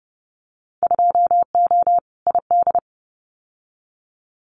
Morse code, heard as a keyed tone, about 30 wpm, 700 Hz. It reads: 2O ID
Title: Winter Field Day Exchanges 30wpm